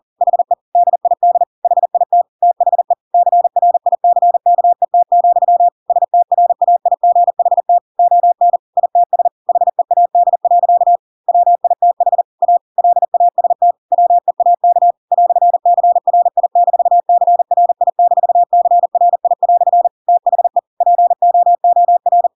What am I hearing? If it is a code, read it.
Morse code, heard as a keyed tone, about 40 wpm, 700 Hz. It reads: HE DID HIT THE CRICKET, STRAIGHT ON ITS HEAD. WITH A LAST WEAK "CRI-CRI-CRI" THE POOR
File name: Pinocchio / Pinocchio ch4 40 40 ik1hge